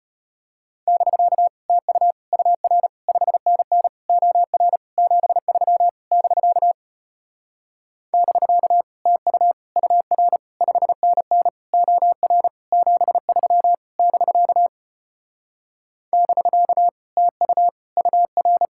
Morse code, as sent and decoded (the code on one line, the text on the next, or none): -...-.- / - ..- / ..- .-. / ..... -. -. / --- .-. / --... ...-- / -...-.- / -...-.- / - ..- / ..- .-. / ..... -. -. / --- .-. / --... ...-- / -...-.- / -...-.- / - ..- / ..- .-.
<BK> TU UR 5NN OR 73 <BK> <BK> TU UR 5NN OR 73 <BK> <BK> TU UR